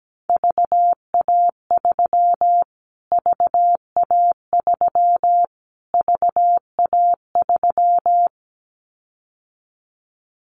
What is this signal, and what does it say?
Morse code, heard as a keyed tone, about 17 wpm, 700 Hz. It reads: VA3 VA3 VA3
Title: Callsign Prefixes for Top 25 DXCC - Repeated 3x - 17wpm